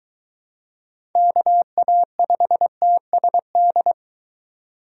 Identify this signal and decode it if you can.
Morse code, heard as a keyed tone, about 23 wpm, 700 Hz. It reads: KA5TSD